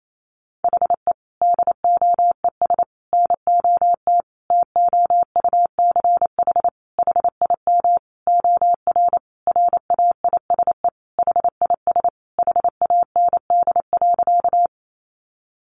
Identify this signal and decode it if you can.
Morse code, heard as a keyed tone, about 28 wpm, 700 Hz. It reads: HE DOES NOT TOUCH HIM OR RAISE HIS HAND.
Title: Sentences from Top 1400 Words 28wpm